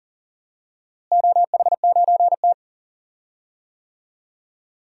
Morse code, heard as a keyed tone, about 40 wpm, 700 Hz. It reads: OH9T
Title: DXCC-100 Call Signs - 4 Characters - Repeated 2x - 40wpm